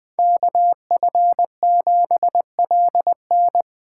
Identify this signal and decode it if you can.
Morse code, heard as a keyed tone, about 24 wpm, 700 Hz. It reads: KF7LN
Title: Speed Racing - USA 2x2 Call Signs 20wpm